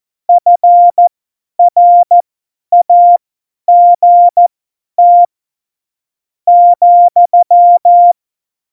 Morse code, heard as a keyed tone, about 14 wpm, spaced as for 9 wpm, 700 Hz. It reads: FRAGT,
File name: Morsekurs / Witze / Witz 14. Januar 2020, 50 F70